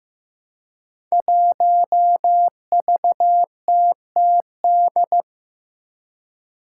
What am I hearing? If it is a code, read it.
Morse code, heard as a keyed tone, about 15 wpm, 700 Hz. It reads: 1VTTD
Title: Five Challenging Letter-Number Combinations 15wpm